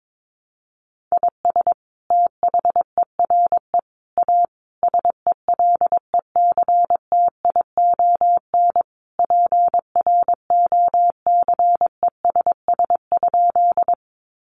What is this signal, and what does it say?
Morse code, heard as a keyed tone, about 22 wpm, 700 Hz. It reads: IS THERE A SELECTION PROCESS?
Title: Sentences from Top 1400 Words 22wpm